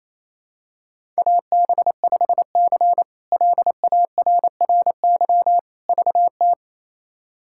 Morse code, heard as a keyed tone, about 28 wpm, 700 Hz. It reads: AB5C LARRY VT
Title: K1USN SST QSO Practice - 28wpm